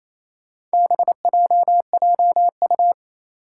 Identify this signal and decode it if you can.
Morse code, heard as a keyed tone, about 28 wpm, 700 Hz. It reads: BJJU
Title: Four Challenging Letter-Number Combinations 28wpm